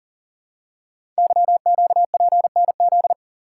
Morse code, heard as a keyed tone, about 40 wpm, 700 Hz. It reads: YQPNZ
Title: Code Groups - 5 Letters - 40wpm